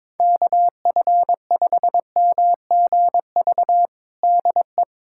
Morse code, heard as a keyed tone, about 22 wpm, 700 Hz. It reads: KF5MGV DE
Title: USA Callsign DE Callsign 22wpm